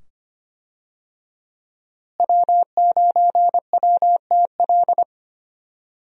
Morse code, heard as a keyed tone, about 25 wpm, 700 Hz. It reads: W9WTL
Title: USA Call Signs - 5 Characters - Repeated 2x - 25wpm